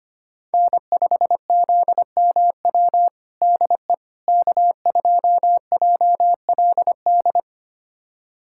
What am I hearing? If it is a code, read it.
Morse code, heard as a keyed tone, about 25 wpm, 700 Hz. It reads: N5ZMW DE K2JLD
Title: USA Callsign DE Callsign - 5 Characters - 25wpm